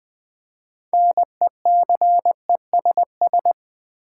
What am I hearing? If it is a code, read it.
Morse code, heard as a keyed tone, about 20 wpm, 700 Hz. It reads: NECESS
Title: Build A Word - 3 Letters At A Time - 9-Letter Words - 20wpm